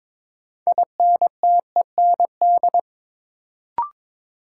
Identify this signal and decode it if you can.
Morse code, heard as a keyed tone, about 22 wpm, 700 Hz. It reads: INTEND
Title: Sentences from Top 1600 Words - New Word Drill Alt - 22wpm